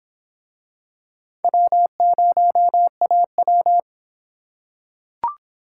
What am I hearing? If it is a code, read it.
Morse code, heard as a keyed tone, about 25 wpm, 700 Hz. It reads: W0AW